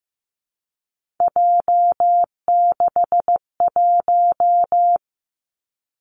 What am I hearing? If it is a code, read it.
Morse code, heard as a keyed tone, about 15 wpm, 700 Hz. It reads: J61
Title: Three Challenging Letter-Number Combinations 15wpm